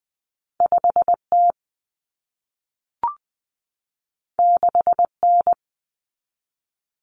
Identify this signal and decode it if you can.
Morse code, heard as a keyed tone, about 20 wpm, 700 Hz. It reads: 5T 6N
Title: Common Cut Numbers - 2-Digit 20wpm